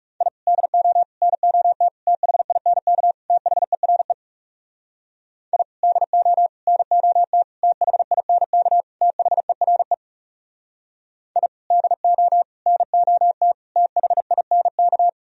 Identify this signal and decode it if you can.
Morse code, heard as a keyed tone, about 40 wpm, 700 Hz. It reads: I DO NOT THINK THERE I DO NOT THINK THERE I DO NOT THINK